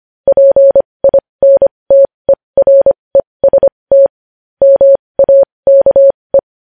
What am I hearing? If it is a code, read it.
Morse code, heard as a keyed tone, about 25 wpm, spaced as for 20 wpm, 550 Hz. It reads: PINTEREST MAKE
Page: News headlines in morse code at 20 wpm